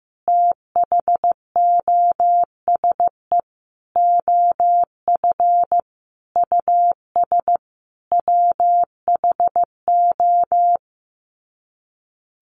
Morse code, heard as a keyed tone, about 15 wpm, 700 Hz. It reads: THOSE OF US WHO
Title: Sets of 5 Words Encore 15wpm